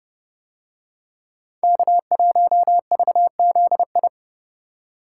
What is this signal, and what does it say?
Morse code, heard as a keyed tone, about 29 wpm, 700 Hz. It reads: K1VZI